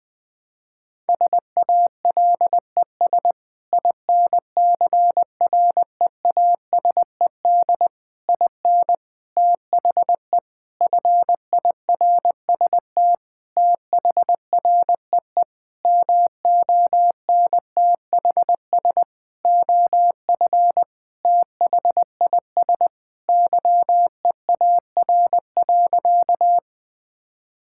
Morse code, heard as a keyed tone, about 20 wpm, 700 Hz. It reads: SALES INCREASED IN THE FIRST THREE MONTHS OF THIS YEAR.
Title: Sentences with 8-Word Phrases - Spoken Beginning - 20wpm